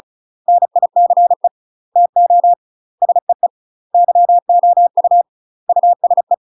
Morse code, heard as a keyed tone, about 34 wpm, 700 Hz. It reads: NICE TO SEE YOU USE
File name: UK MASTER QSOs TRAINING / 35WPM / UK MASTER QSOs 0001